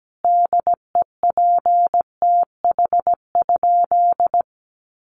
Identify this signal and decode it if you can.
Morse code, heard as a keyed tone, about 17 wpm, 700 Hz. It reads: DEPTH?
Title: Sentences from Top 1200 Words 17wpm